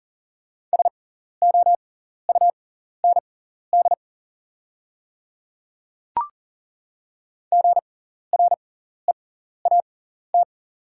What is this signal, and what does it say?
Morse code, heard as a keyed tone, about 40 wpm, spaced as for 14 wpm, 690 Hz. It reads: SOUND GREAT
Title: 6x Character Spacing - Top 500 Words - 40wpm